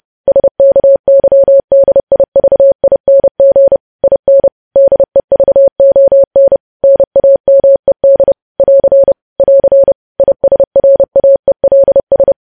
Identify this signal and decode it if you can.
Morse code, heard as a keyed tone, about 30 wpm, 550 Hz. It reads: SKYDIVING IN DEVON NAMED <AR> <AR> ISRAELS
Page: news headlines in morse code at 30 wpm